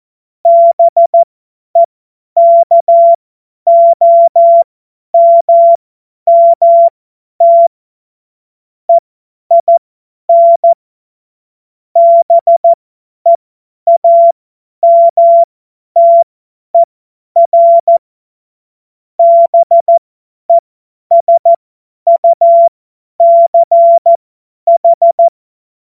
Morse code, heard as a keyed tone, about 14 wpm, spaced as for 10 wpm, 680 Hz. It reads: BEKOMMT EIN BEAMTER BESUCH
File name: Witz-14-Oktober-2025-50-F70